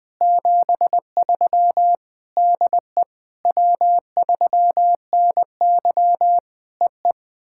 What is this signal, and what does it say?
Morse code, heard as a keyed tone, about 19 wpm, 700 Hz. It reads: 73 DE W3NY EE
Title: POTA QSO Practice - 20wpm